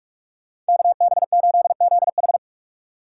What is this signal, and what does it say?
Morse code, heard as a keyed tone, about 54 wpm, 700 Hz. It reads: KB8ZH
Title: Speed Racing - USA Call Signs 45wpm